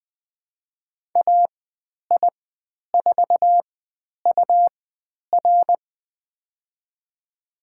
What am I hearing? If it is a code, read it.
Morse code, heard as a keyed tone, about 20 wpm, spaced as for 10 wpm, 700 Hz. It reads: AI4UR
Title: 109 Lesson 18g - Letter m - Call Signs 20wpm